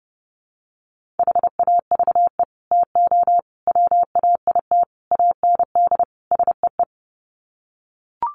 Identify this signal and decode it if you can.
Morse code, heard as a keyed tone, about 30 wpm, 700 Hz. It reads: HAVE TO WAIT AND SEE
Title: Sets of 5 Words Encore2 30wpm